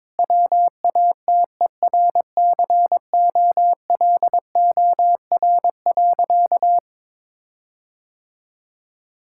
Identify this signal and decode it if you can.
Morse code, heard as a keyed tone, about 22 wpm, 700 Hz. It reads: WATERCOLOR.
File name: Difficult Sentences with Hints - 22wpm